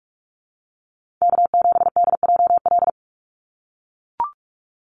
Morse code, heard as a keyed tone, about 44 wpm, 700 Hz. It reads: K7DJL
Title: NA SOTA Chaser Callsigns 45wpm